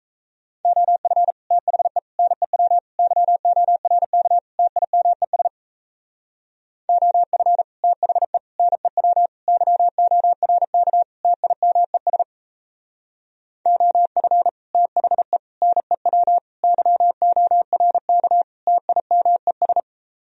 Morse code, heard as a keyed tone, about 34 wpm, 700 Hz. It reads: OF THE NEW YORK TIMES OF THE NEW YORK TIMES OF THE NEW YORK TIMES
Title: Speed-Racing - Sets of 5 Words 28wpm